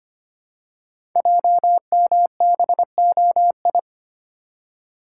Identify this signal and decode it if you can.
Morse code, heard as a keyed tone, about 25 wpm, 700 Hz. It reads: JMBOI
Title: Code Groups - 5 Letters - 25wpm